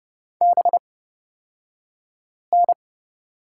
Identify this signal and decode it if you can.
Morse code, heard as a keyed tone, about 30 wpm, spaced as for 5 wpm, 700 Hz. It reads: BN
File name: Lesson 36d - BT - Two Characters 30wpm